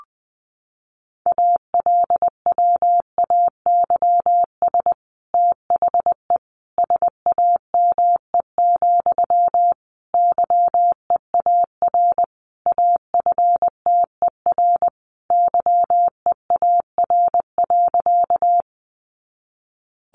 Morse code, heard as a keyed tone, about 20 wpm, 700 Hz. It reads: ALWAYS THE SAME, YEAR AFTER YEAR.
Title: Sentences from Top 300 Words 20wpm